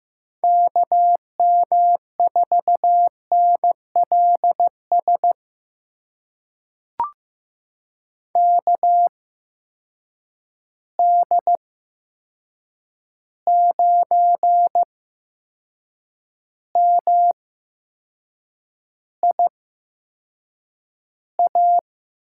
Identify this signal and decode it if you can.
Morse code, heard as a keyed tone, about 15 wpm, spaced as for 6 wpm, 700 Hz. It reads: KM4NLS KD9MIA
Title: USA Call Signs - 6 Characters - 8x Character Spacing - 15wpm